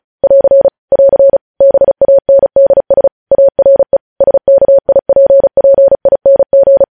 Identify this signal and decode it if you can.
Morse code, heard as a keyed tone, about 36 wpm, 550 Hz. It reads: <AR> <AR> BANDS ARE SKIPPING